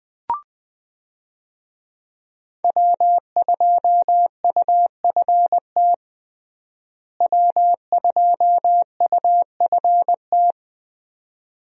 Morse code, heard as a keyed tone, about 20 wpm, 700 Hz. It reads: W2UFT W2UFT
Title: USA Call Signs - 5 Characters - Repeated 3x - 20wpm